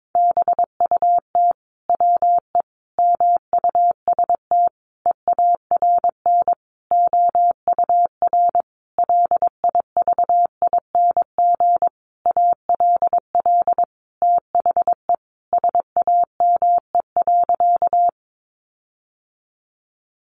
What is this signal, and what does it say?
Morse code, heard as a keyed tone, about 22 wpm, 700 Hz. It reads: BUT WE MUST EARN OUR LIVING ALL THE SAME.
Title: Sentences from Top 1500 Words 22wpm